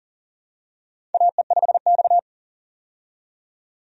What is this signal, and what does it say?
Morse code, heard as a keyed tone, about 37 wpm, 700 Hz. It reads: AE5X
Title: USA Call Signs - 4 Characters - 40wpm